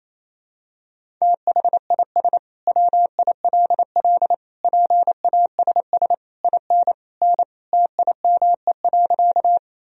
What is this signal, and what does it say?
Morse code, heard as a keyed tone, about 28 wpm, 700 Hz. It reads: THIS WILL PASS IN N TIME.
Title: Sentences from Top 400 Words and Abbreviations 28wpm